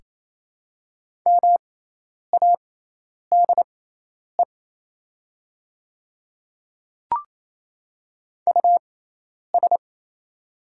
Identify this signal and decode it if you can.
Morse code, heard as a keyed tone, about 28 wpm, spaced as for 10 wpm, 700 Hz. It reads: MADE US
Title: 6x Character Spacing - Top 500 Words - 28wpm